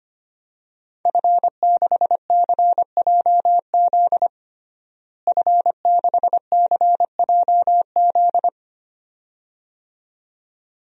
Morse code, heard as a keyed tone, about 25 wpm, 700 Hz. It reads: F6CJZ F6CJZ
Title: DXCC-100 Call Signs - 5 Characters - Repeated 2x - 25wpm